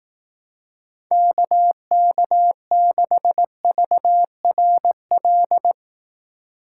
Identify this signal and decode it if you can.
Morse code, heard as a keyed tone, about 19 wpm, 700 Hz. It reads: KK6VRL
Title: Speed Racing - USA 2x3 Call Signs 15wpm